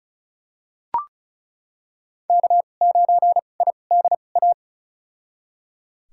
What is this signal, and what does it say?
Morse code, heard as a keyed tone, about 36 wpm, spaced as for 25 wpm, 700 Hz. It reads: K9IDA
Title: USA Call Signs - 5 Characters - 2x Character Spacing - 35wpm